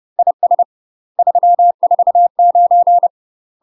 Morse code, heard as a keyed tone, about 30 wpm, 700 Hz. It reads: IS 349
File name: UK MASTER QSOs TRAINING / 30WPM / UK MASTER QSOs 0035